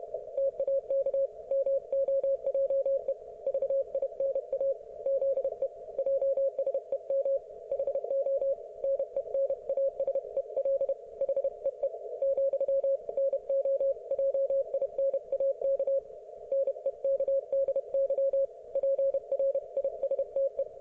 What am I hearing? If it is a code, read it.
Morse code, heard as a keyed tone, about 31 wpm, 530 Hz. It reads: TAK MOJE VINA ZE JSEM HO NENASEL HEE , NO JINAK NEKDY PRISTE